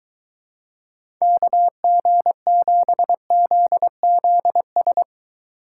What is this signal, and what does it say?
Morse code, heard as a keyed tone, about 22 wpm, 700 Hz. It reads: KG7ZZS